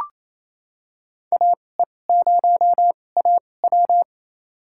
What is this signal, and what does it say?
Morse code, heard as a keyed tone, about 28 wpm, spaced as for 20 wpm, 700 Hz. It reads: AE0AW
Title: USA Call Signs - 5 Characters - 2x Character Spacing - 28wpm